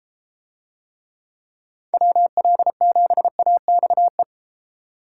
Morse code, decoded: WL7AXE

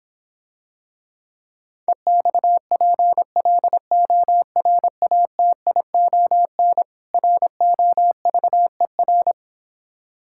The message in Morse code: EXPLORATION ROVER

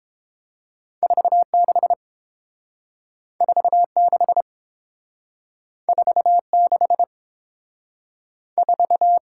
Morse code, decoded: 46 46 46 4

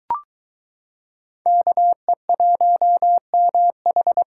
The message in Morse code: KE1MH

